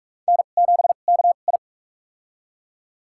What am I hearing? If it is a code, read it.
Morse code, heard as a keyed tone, about 42 wpm, spaced as for 31 wpm, 690 Hz. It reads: N7KI
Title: USA Call Signs - 4 Characters - 2x Character Spacing - 45wpm